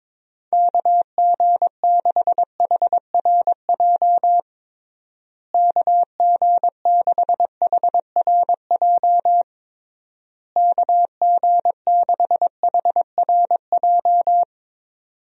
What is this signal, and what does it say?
Morse code, heard as a keyed tone, about 22 wpm, 700 Hz. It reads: KG6HRJ KG6HRJ KG6HRJ
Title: USA Call Signs - 6 Characters - Repeated 3x - 22wpm